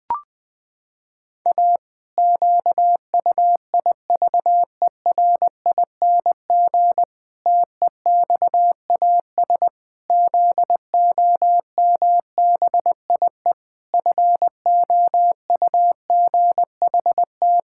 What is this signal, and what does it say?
Morse code, heard as a keyed tone, about 20 wpm, 700 Hz. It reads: A QUIVERING TEXAS ZOMBIE FOUGHT
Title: Pangram Challenge v2 - 20wpm